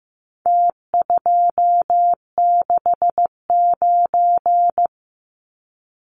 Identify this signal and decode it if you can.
Morse code, heard as a keyed tone, about 15 wpm, 700 Hz. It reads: T269
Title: Numbers - Contest Style - 4-Digit 15wpm